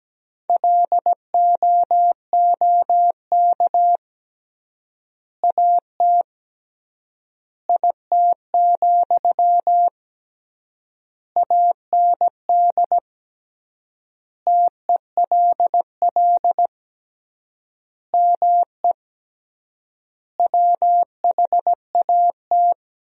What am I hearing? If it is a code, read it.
Morse code, heard as a keyed tone, about 17 wpm, 700 Hz. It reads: LOOK AT IT, AND TELL ME WHAT
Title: 3x Word-Spacing - Sentences from Top 200 Words - 17wpm